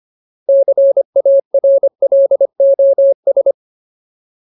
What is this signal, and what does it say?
Morse code, heard as a keyed tone, about 25 wpm, 550 Hz. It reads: CARLOS
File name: Ultimate Word List / 25WPM Ultimate